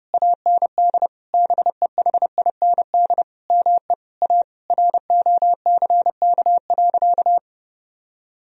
Morse code, heard as a keyed tone, about 32 wpm, 700 Hz. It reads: AND BEHIND ME A ROCK.
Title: Sentences from Top 1000 Words - Review Part 2 - 30wpm